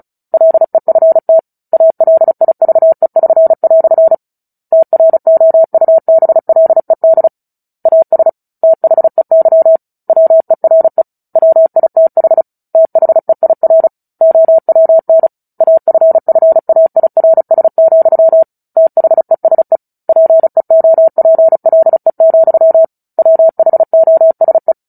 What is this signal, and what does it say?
Morse code, heard as a keyed tone, about 35 wpm, 650 Hz. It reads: LEFT ALIVE<SN>" TROUBLED AS THEY WERE WITH THEIR OWN AFFAIRS, THESE PEOPLE, WHOSE
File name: Book Two / 09-Chapter 9 - 35 WPM